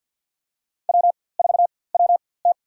Morse code, heard as a keyed tone, about 49 wpm, spaced as for 23 wpm, 700 Hz. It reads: W4WT
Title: USA Call Signs - 4x Character Spacing - 50wpm